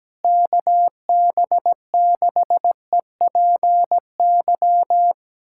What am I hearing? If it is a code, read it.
Morse code, heard as a keyed tone, about 17 wpm, 700 Hz. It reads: KB6EPY
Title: USA Callsign DE Callsign 17wpm